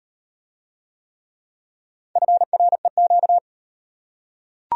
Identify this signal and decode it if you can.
Morse code, heard as a keyed tone, about 34 wpm, 700 Hz. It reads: FREQ